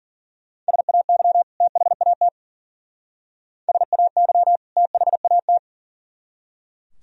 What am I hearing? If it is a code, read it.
Morse code, heard as a keyed tone, about 48 wpm, 700 Hz. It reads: SAY THAT SAY THAT
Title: Speed-Racing - Sets of 2 Words 40wpm